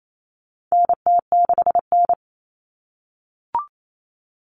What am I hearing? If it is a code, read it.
Morse code, heard as a keyed tone, about 28 wpm, 700 Hz. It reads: NT6N
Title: All Cut Numbers - 4-Digit 28wpm